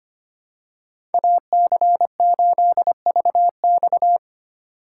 Morse code, heard as a keyed tone, about 25 wpm, 700 Hz. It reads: AC8VX
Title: USA Call Signs - 6x Character Spacing - 25wpm